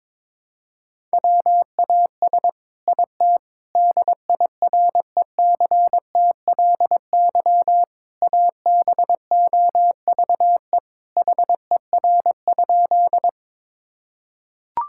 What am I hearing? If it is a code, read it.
Morse code, heard as a keyed tone, about 21 wpm, 700 Hz. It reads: WAS IT DIRECTLY ABOVE HER?
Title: Sentences from Top 1000 Words - Review Part 5 - 22wpm